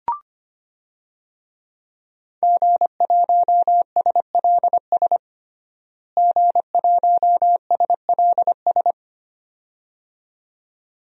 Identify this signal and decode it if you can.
Morse code, heard as a keyed tone, about 25 wpm, 700 Hz. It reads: G1SLS G1SLS
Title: DXCC-100 Call Signs - 5 Characters - Repeated 2x - 25wpm